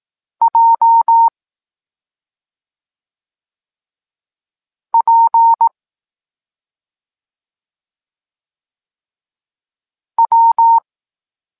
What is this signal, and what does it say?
Morse code, heard as a keyed tone, about 18 wpm, 920 Hz. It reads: J P W